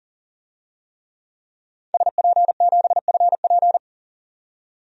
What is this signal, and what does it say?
Morse code, heard as a keyed tone, about 40 wpm, 700 Hz. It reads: SP7FP
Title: DXCC-100 Call Signs - 5 Characters - 40wpm